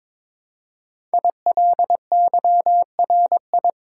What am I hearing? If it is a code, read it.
Morse code, heard as a keyed tone, about 22 wpm, 700 Hz. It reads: ILYRI
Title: Code Groups - 5 Letters - 22wpm